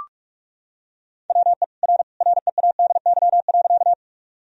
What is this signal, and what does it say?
Morse code, heard as a keyed tone, about 46 wpm, 700 Hz. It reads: WE R READY.
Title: Sentences from Top 400 Words and Abbreviations 45wpm